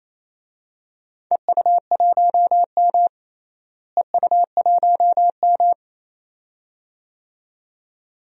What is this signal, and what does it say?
Morse code, heard as a keyed tone, about 28 wpm, 700 Hz. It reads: EU1M EU1M
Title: DXCC-100 Call Signs - 4 Characters - Repeated 2x - 28wpm